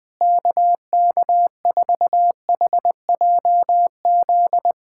Morse code, heard as a keyed tone, about 21 wpm, 700 Hz. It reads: KK4HJZ
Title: USA Callsign DE Callsign 20wpm